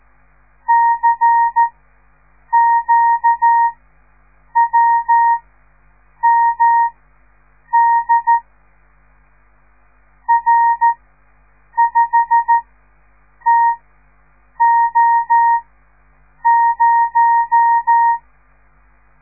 Morse code, heard as a keyed tone, about 13 wpm, spaced as for 7 wpm, 930 Hz. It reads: CQWMD R5TO0
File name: radiomorningstar / morse / les 6